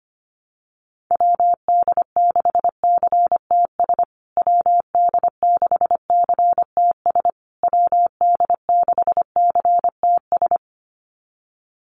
Morse code, heard as a keyed tone, about 25 wpm, 700 Hz. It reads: WD6CTS WD6CTS WD6CTS
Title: USA Call Signs Repeated 3x 25wpm